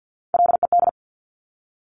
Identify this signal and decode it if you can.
Morse code, heard as a keyed tone, about 50 wpm, 700 Hz. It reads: LED
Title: Top Three Letter Combinations in Words 50wpm